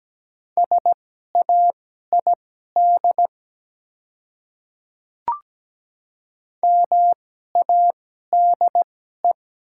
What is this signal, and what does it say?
Morse code, heard as a keyed tone, about 17 wpm, spaced as for 12 wpm, 700 Hz. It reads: SAID MADE
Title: 2x Character Spacing - Top 300 Words - 17wpm